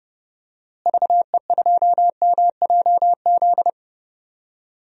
Morse code, heard as a keyed tone, about 30 wpm, 700 Hz. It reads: VE2MJZ
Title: DXCC-100 Call Signs - 6 Characters - Repeated 2x - 30wpm